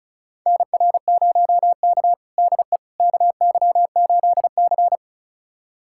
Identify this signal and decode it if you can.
Morse code, heard as a keyed tone, about 36 wpm, 700 Hz. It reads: NR0K DE KY8C
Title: USA Callsign DE Callsign - 4 Characters - 35wpm